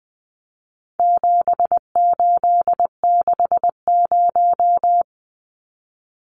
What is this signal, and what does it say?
Morse code, heard as a keyed tone, about 20 wpm, 700 Hz. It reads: 7860
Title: Numbers - 4-Digit 20wpm